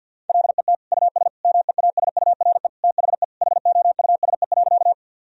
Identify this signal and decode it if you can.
Morse code, heard as a keyed tone, about 48 wpm, 700 Hz. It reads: LET US MEASURE THE HOUSE.
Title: Sentences from Top 1000 Words - Review Part 2 - 50wpm